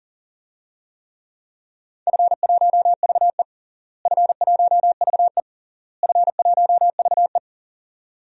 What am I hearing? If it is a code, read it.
Morse code, heard as a keyed tone, about 40 wpm, 700 Hz. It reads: F1VE F1VE F1VE
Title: DXCC-100 Call Signs - 4 Characters - Repeated 3x - 40wpm